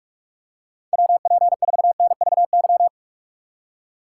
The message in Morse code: WP4NVY

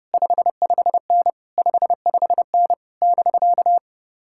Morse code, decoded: 55N 55N <BK>

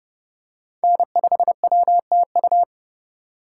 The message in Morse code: N5WTU

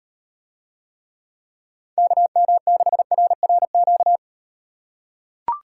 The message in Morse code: KM6RRQ

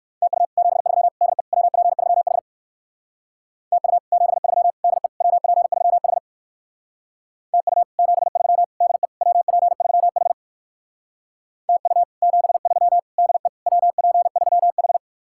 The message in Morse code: TU 73 DE WP3H TU 73 DE WP3H TU 73 DE WP3H TU 73 DE WP3H

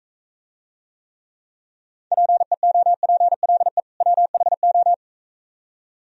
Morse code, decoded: PEOPLE WHO

